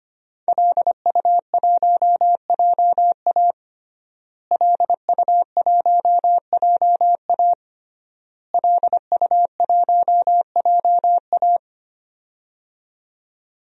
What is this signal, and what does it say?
Morse code, heard as a keyed tone, about 25 wpm, 700 Hz. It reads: LU1JA LU1JA LU1JA
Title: DXCC-100 Call Signs - 5 Characters - Repeated 3x - 25wpm